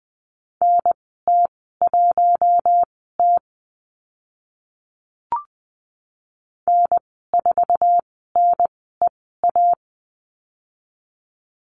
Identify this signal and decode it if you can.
Morse code, heard as a keyed tone, about 20 wpm, spaced as for 14 wpm, 700 Hz. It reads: NT1T N4NEA
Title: 048 Lesson 9g - Number 4 - Call Signs 20wpm